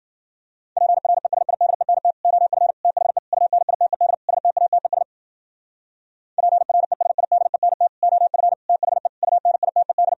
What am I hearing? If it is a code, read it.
Morse code, heard as a keyed tone, about 50 wpm, 700 Hz. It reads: PRESIDENT OF THE UNITED STATES PRESIDENT OF THE UNITED